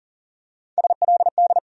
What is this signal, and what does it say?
Morse code, heard as a keyed tone, about 40 wpm, 700 Hz. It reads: SLD
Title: USA Call Sign Postfixes - 3 Letters 40wpm